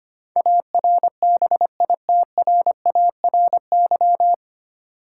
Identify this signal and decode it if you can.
Morse code, heard as a keyed tone, about 25 wpm, 700 Hz. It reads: ARBITRARY
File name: Nine Letter Words - 25wpm